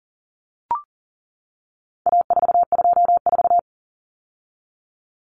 Morse code, heard as a keyed tone, about 40 wpm, 700 Hz. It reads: A424
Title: Common Cut Numbers - 4-Digit 40wpm